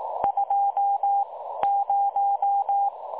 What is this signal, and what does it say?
Morse code, heard as a keyed tone, about 18 wpm, spaced as for 13 wpm, 810 Hz. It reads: 20
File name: dlazaawansowanych / nagraniazpasma / websdr recording 3565.0kHz